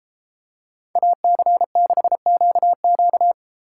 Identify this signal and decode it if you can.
Morse code, heard as a keyed tone, about 31 wpm, 700 Hz. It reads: AC6QQ